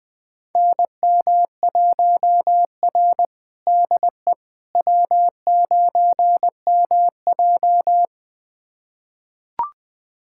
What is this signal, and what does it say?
Morse code, heard as a keyed tone, about 20 wpm, 700 Hz. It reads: NM1R DE W9MJ
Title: USA Callsign DE Callsign - 4 Characters - 20wpm